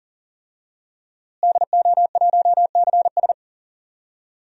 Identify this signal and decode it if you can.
Morse code, heard as a keyed tone, about 40 wpm, 700 Hz. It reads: DO1CS
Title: DXCC-100 Call Signs - 5 Characters - Repeated 3x - 40wpm